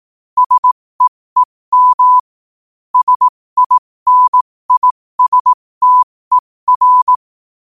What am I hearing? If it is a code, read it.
Morse code, heard as a keyed tone, about 18 wpm, spaced as for 15 wpm, 1000 Hz. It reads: SEEM SINISTER
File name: qotdfiles / 15wpm